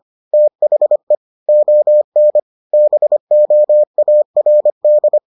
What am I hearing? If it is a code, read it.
Morse code, heard as a keyed tone, about 25 wpm, 600 Hz. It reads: THE ON BOARD